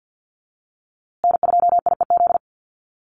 Morse code, heard as a keyed tone, about 50 wpm, 700 Hz. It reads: N2IEZ
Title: USA Call Signs Repeated 3x 50wpm